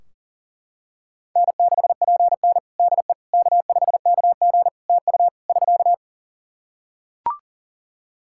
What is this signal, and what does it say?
Morse code, heard as a keyed tone, about 43 wpm, 700 Hz. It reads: N6PN DE K5KG TU <SK>
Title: Practice QSO - Exchange 4 - Ending - 40wpm